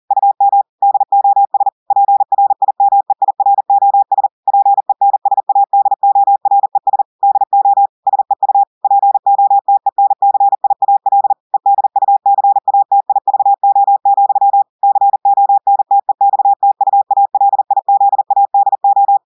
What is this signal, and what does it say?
Morse code, heard as a keyed tone, about 40 wpm, 800 Hz. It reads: UM DOS PRIMEIROS PENSADORES DO SEU POTENCIAL EDUCATIVO, CONTEXTUALIZADO
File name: audios / portugais / portugais texte en clair 1 vitesse 40 mots minutes